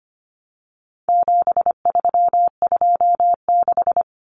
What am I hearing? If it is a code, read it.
Morse code, heard as a keyed tone, about 25 wpm, 700 Hz. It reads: 7326
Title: Common Cut Numbers - 4-Digit 25wpm